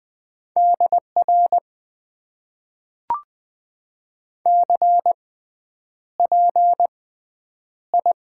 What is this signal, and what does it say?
Morse code, heard as a keyed tone, about 20 wpm, spaced as for 9 wpm, 700 Hz. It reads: DR CPI
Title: Top 90 QSO Elements - 6x Character Spacing - 20wpm